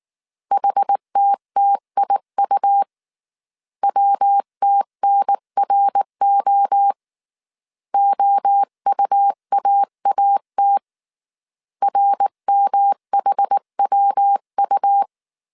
Morse code, heard as a keyed tone, about 19 wpm, 800 Hz. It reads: HTTIU WTNRO OUAAT RMHWU
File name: CW / DET / 1 3 1500C S B 18wpm 1916